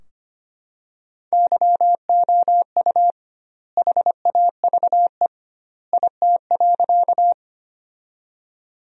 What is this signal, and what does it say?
Morse code, heard as a keyed tone, about 25 wpm, 700 Hz. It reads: YOU HAVE IT.
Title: 2x Word-Spacing - Sentences from Top 100 Words - 25wpm